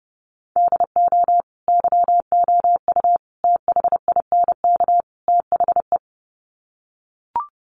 Morse code, heard as a keyed tone, about 30 wpm, 700 Hz. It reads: DO YOU THINK THE
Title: Sets of 4 Words 30wpm